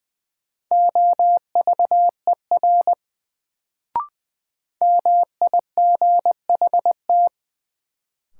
Morse code, heard as a keyed tone, about 20 wpm, 700 Hz. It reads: OVER MIGHT
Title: Sets of 2 Words - New Word Drill - 20wpm